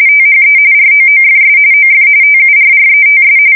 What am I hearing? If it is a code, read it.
RTTY, about 45.45 baud. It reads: THE ROOFS OF TURRETS
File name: fsk170 / USS-Missouri